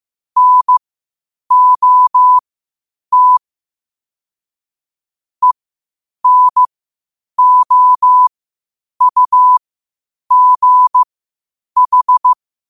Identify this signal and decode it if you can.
Morse code, heard as a keyed tone, about 15 wpm, spaced as for 8 wpm, 1000 Hz. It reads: NOT ENOUGH
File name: qotdfiles / 8wpm